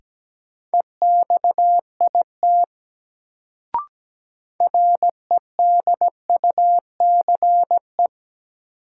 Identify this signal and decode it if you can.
Morse code, heard as a keyed tone, about 17 wpm, 700 Hz. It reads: EXIT REDUCE
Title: Sentences from Top 1100 Words - New Word Drill Alt - 17wpm